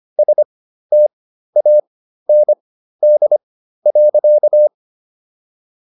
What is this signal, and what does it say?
Morse code, heard as a keyed tone, about 25 wpm, spaced as for 13 wpm, 600 Hz. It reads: STAND.